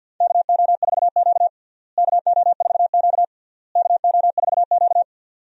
Morse code, heard as a keyed tone, about 49 wpm, 700 Hz. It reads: KO4Q KO4Q KO4Q
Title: USA Call Signs - 4 Characters - Repeated 3x - 50wpm